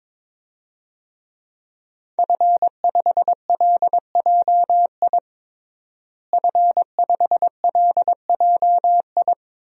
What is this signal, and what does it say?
Morse code, heard as a keyed tone, about 22 wpm, 700 Hz. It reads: F5LJI F5LJI
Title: DXCC-100 Call Signs - 5 Characters - Repeated 3x - 22wpm